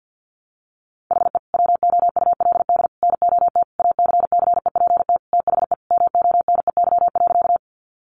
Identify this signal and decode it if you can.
Morse code, heard as a keyed tone, about 50 wpm, 700 Hz. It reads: HE WOULD NOT ACCEPT THE MONEY.
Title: Sentences from Top 1300 Words 50wpm